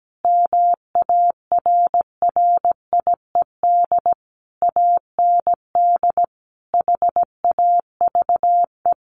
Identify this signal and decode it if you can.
Morse code, heard as a keyed tone, about 17 wpm, 700 Hz. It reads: MARRIED AND HAVE
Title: Easy Phrases 17wpm